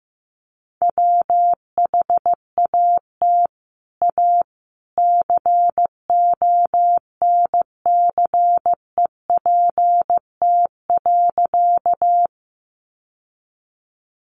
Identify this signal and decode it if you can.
Morse code, heard as a keyed tone, about 15 wpm, 700 Hz. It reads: WHAT A CONCEPT.
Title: Sentences from Top 1100 Words 15wpm